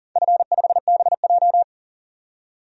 Morse code, heard as a keyed tone, about 42 wpm, 700 Hz. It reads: F5BJ
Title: DXCC-100 Call Signs - 4 Characters - 40wpm